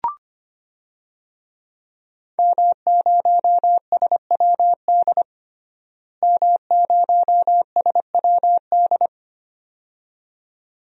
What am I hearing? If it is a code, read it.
Morse code, heard as a keyed tone, about 25 wpm, 700 Hz. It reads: M0SWD M0SWD
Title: DXCC-100 Call Signs - 5 Characters - Repeated 2x - 25wpm